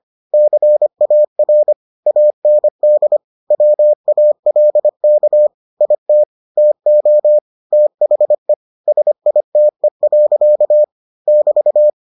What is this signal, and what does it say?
Morse code, heard as a keyed tone, about 25 wpm, 600 Hz. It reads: CAR AND WALK IT TO THE SITE. <BT>